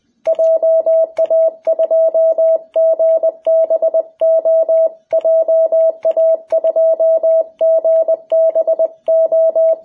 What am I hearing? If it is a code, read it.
Morse code, heard as a keyed tone, about 20 wpm, 660 Hz. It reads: JA2GBO JA2GBO
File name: jh8cbh / 2021cw